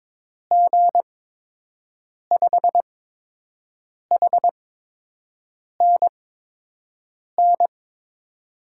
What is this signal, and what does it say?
Morse code, heard as a keyed tone, about 22 wpm, spaced as for 6 wpm, 700 Hz. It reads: G5HNN
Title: USA Call Signs - 6 Characters - 8x Character Spacing - 22wpm